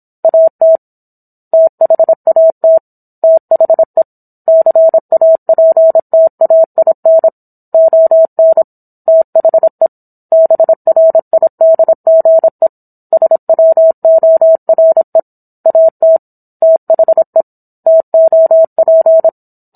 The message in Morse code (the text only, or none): AT THAT THE CAPTAIN ON THE BRIDGE SWORE AT THE TOP